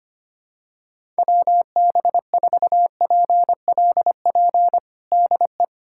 Morse code, heard as a keyed tone, about 25 wpm, 700 Hz. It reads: WB4PLP DE
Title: USA Callsign DE Callsign 25wpm